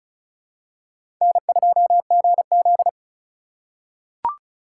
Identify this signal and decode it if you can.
Morse code, heard as a keyed tone, about 34 wpm, 700 Hz. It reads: N2GZ
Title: USA Call Signs - 4 Characters - 4x Character Spacing - 35wpm